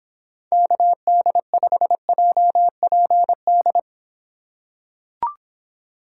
Morse code, heard as a keyed tone, about 26 wpm, 700 Hz. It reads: KD5JPD